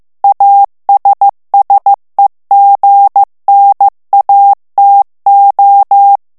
Morse code, heard as a keyed tone, about 15 wpm, 800 Hz. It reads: ASSEGNATO
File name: morse mp3 / ITALIEN texte clair 2 vitesse 14 mots minute